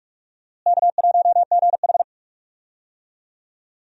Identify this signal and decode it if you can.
Morse code, heard as a keyed tone, about 46 wpm, 700 Hz. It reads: K1GH
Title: USA Call Signs - 4 Characters - Repeated 3x - 45wpm